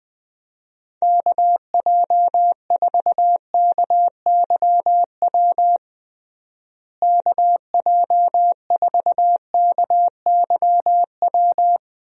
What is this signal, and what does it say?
Morse code, heard as a keyed tone, about 20 wpm, 700 Hz. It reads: KJ4KYW KJ4KYW
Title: USA Call Signs - 6 Characters - Repeated 3x - 20wpm